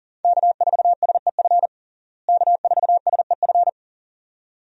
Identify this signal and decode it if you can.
Morse code, heard as a keyed tone, about 40 wpm, 700 Hz. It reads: K4SEF K4SEF
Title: USA Call Signs - 5 Characters - Repeated 2x - 40wpm